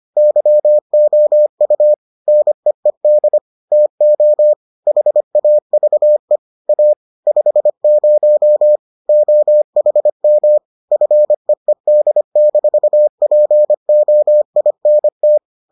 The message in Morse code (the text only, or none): YOU NEED TO HAVE A 50 OHM FEED-POINT